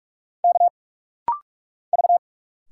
Morse code, heard as a keyed tone, about 46 wpm, 700 Hz. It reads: K V
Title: Single Letters - Rapid-Fire 45wpm